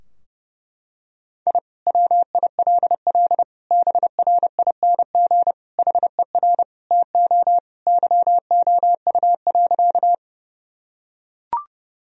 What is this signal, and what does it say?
Morse code, heard as a keyed tone, about 30 wpm, 700 Hz. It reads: I WILL BRING HER TO YOU.
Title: Sentences from Top 1000 Words - Review Part 3 - 30wpm